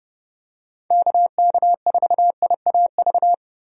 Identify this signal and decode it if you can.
Morse code, heard as a keyed tone, about 32 wpm, 700 Hz. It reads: KK4IAV
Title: USA Call Signs - 6 Characters - 30wpm